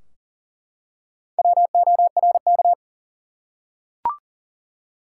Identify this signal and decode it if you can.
Morse code, heard as a keyed tone, about 40 wpm, 700 Hz. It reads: WORK
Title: Four Letter Words - 40wpm